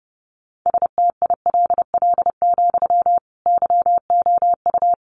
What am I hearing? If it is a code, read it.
Morse code, heard as a keyed tone, about 30 wpm, 700 Hz. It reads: STILL, YOU
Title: Sentences from Top 300 Words 30wpm